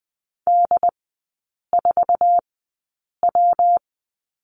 Morse code, heard as a keyed tone, about 20 wpm, spaced as for 8 wpm, 700 Hz. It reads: D4W
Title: 178 Lesson 29e - Slash - Three Characters 20wpm